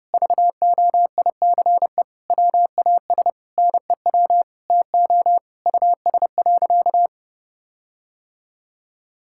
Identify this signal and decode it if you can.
Morse code, heard as a keyed tone, about 32 wpm, 700 Hz. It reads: VOICE WAS NEW TO US.
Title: Sentences from Top 1600 Words 30wpm